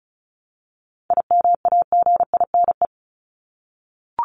Morse code, heard as a keyed tone, about 34 wpm, 700 Hz. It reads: IMAGINE
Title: Top 1251-1500 Words 35wpm